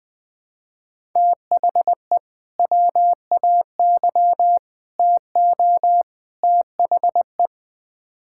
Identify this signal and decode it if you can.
Morse code, heard as a keyed tone, about 20 wpm, 700 Hz. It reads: THE WAY TO THE
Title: Build A Phrase - 4-Word Phrases - 20wpm